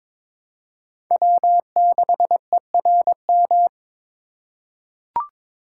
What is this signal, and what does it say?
Morse code, heard as a keyed tone, about 22 wpm, 700 Hz. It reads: W6ERM
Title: USA Call Signs - 5 Characters - 4x Character Spacing - 22wpm